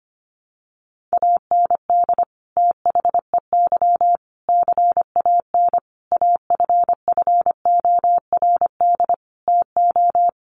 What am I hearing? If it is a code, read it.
Morse code, heard as a keyed tone, about 25 wpm, 700 Hz. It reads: AND THEY CAN AFFORD TO
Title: Sentences from Top 1100 Words 25wpm